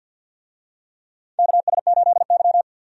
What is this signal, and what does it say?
Morse code, heard as a keyed tone, about 47 wpm, 700 Hz. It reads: KS8Y
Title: USA Call Signs - 4 Characters - 50wpm